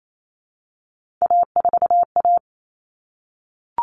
Morse code, heard as a keyed tone, about 28 wpm, 700 Hz. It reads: A4A
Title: All Cut Numbers - 3-Digit 28wpm